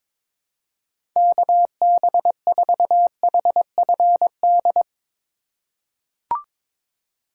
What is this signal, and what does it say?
Morse code, heard as a keyed tone, about 22 wpm, 700 Hz. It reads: KB4HFD
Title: USA Call Signs - 2x Character Spacing - 22wpm